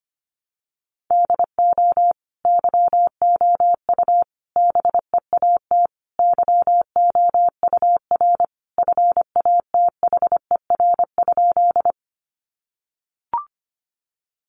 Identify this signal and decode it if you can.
Morse code, heard as a keyed tone, about 25 wpm, 700 Hz. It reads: DO YOU BEAT YOUR FATHER?
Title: Sentences from Top 1300 Words 25wpm